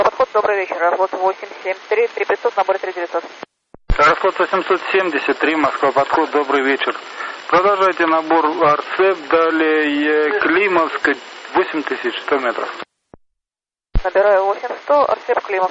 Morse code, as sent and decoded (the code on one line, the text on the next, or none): none